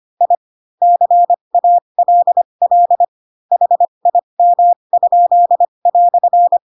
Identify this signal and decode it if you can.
Morse code, heard as a keyed tone, about 25 wpm, spaced as for 22 wpm, 700 Hz. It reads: I CALL HIM?"
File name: Pinocchio / Pinocchio ch3 18 25 ik1hge